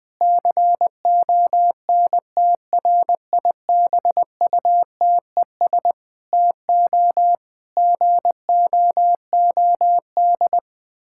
Morse code, heard as a keyed tone, about 20 wpm, 700 Hz. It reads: CONTRIBUTES TO GOOD
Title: Difficult Sentences with Hints - 20wpm